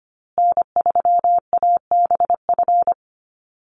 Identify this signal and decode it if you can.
Morse code, heard as a keyed tone, about 25 wpm, 700 Hz. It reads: N3ABF
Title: USA Call Signs Repeated 2x 25wpm